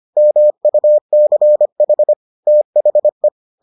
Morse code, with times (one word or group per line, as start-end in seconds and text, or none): 0.2-2.1 MUCH
2.5-3.3 THE